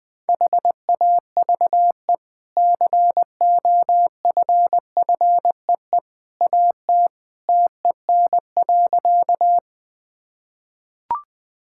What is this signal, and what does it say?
Morse code, heard as a keyed tone, about 21 wpm, 700 Hz. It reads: HAVE COFFEE AT TEN.
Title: Easy Sentences with Hints - 20wpm